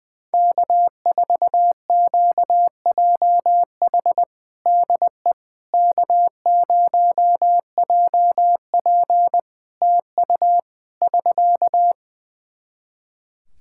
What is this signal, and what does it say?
Morse code, heard as a keyed tone, about 20 wpm, 700 Hz. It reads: K4QJH DE K0JP TU <SK>
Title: Practice QSO - Exchange 4 - Ending - 20wpm